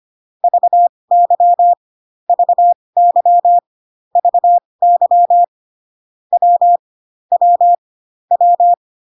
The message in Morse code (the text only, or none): VY VY VY W W W